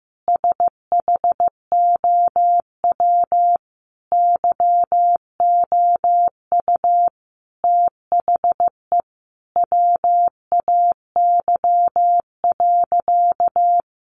Morse code, heard as a keyed tone, about 15 wpm, 700 Hz. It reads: SHOW YOU THE WAY.
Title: Sentences from Top 300 Words 15wpm